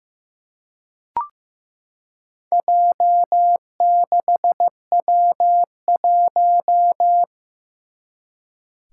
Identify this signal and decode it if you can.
Morse code, heard as a keyed tone, about 15 wpm, 700 Hz. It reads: J6W1
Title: Four Challenging Letter-Number Combinations 15wpm